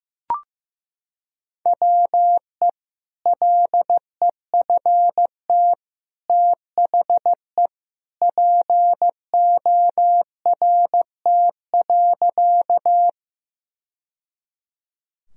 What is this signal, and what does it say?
Morse code, heard as a keyed tone, about 15 wpm, 700 Hz. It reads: WE LEFT THE PORT.
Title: Sentences from Top 1000 Words - Review Part 1 - 15wpm